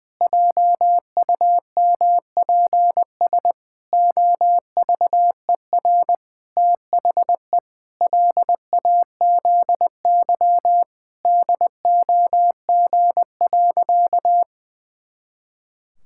Morse code, heard as a keyed tone, about 19 wpm, 700 Hz. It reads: JUMPS OVER THE LAZY DOG.
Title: Pangram Challenge v2 - 20wpm